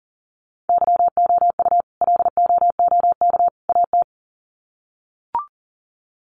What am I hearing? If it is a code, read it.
Morse code, heard as a keyed tone, about 40 wpm, 700 Hz. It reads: YOU LOOK AT
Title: Sets of 3 Words 40wpm